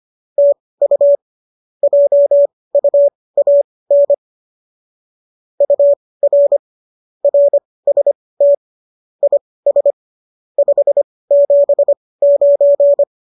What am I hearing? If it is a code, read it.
Morse code, heard as a keyed tone, about 25 wpm, spaced as for 18 wpm, 570 Hz. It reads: TU JUAN UR RST IS 579
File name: qso204 18